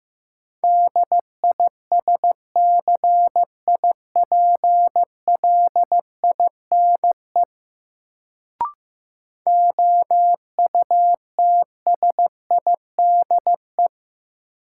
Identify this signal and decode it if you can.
Morse code, heard as a keyed tone, about 15 wpm, 700 Hz. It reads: DISCIPLINE OUTSIDE
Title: Sentences from Top 900 Words - New Word Drill - 15wpm